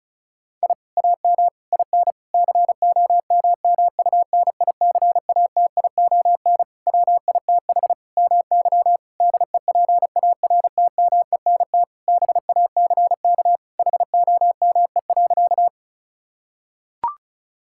Morse code, heard as a keyed tone, about 34 wpm, 700 Hz. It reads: I AM IN COMMUNICATION WITH MY DEPARTMENT BACK HOME.
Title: Sentences from Top 1000 Words - Review Part 6 - 35wpm